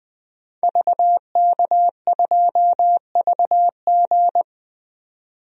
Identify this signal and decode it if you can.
Morse code, heard as a keyed tone, about 20 wpm, 700 Hz. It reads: VK2VG
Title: DXCC-100 Call Signs - 5 Characters - 20wpm